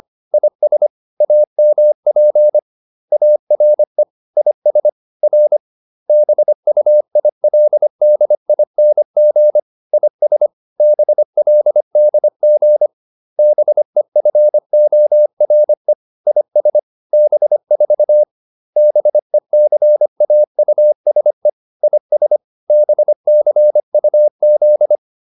Morse code, heard as a keyed tone, about 25 wpm, 600 Hz. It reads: IS AMP ARE IS R BUILDING IS BLDG BEFORE IS B4 BECAUSE IS BCUZ